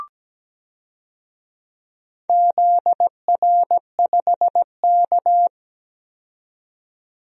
Morse code, heard as a keyed tone, about 17 wpm, 700 Hz. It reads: ZR5K
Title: DXCC-100 Call Signs - 4 Characters - 17wpm